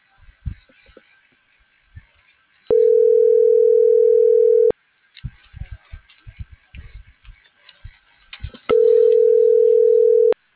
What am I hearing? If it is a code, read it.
Ringback tone (North America), 440 + 480 Hz.